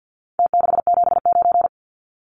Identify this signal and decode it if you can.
Morse code, heard as a keyed tone, about 49 wpm, 700 Hz. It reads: T679
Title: Numbers - Contest Style - 4-Digit 50wpm